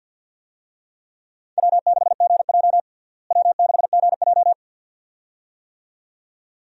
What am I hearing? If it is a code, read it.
Morse code, heard as a keyed tone, about 47 wpm, 700 Hz. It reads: W6GJ W6GJ
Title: USA Call Signs - 4 Characters - Repeated 2x - 50wpm